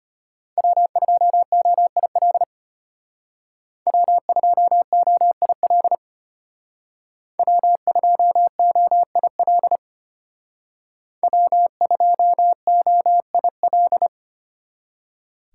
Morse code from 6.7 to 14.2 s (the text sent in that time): W2OIL W2OIL